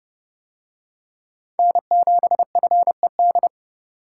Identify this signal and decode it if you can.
Morse code, heard as a keyed tone, about 27 wpm, 700 Hz. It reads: N7FED